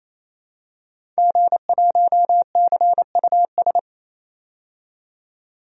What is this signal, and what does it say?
Morse code, heard as a keyed tone, about 27 wpm, 700 Hz. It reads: G1CUS
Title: DXCC-100 Call Signs - 5 Characters - 28wpm